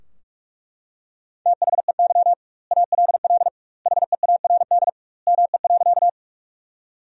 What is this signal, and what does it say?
Morse code, heard as a keyed tone, about 41 wpm, 700 Hz. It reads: THEY ALL HEARD ME.
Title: 2x Word-Spacing - Sentences from Top 400 Words - 45wpm